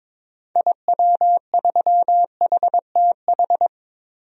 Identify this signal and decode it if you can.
Morse code, heard as a keyed tone, about 22 wpm, 700 Hz. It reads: IW3HTH
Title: DXCC-100 Call Signs - 6 Characters - Repeated 3x - 22wpm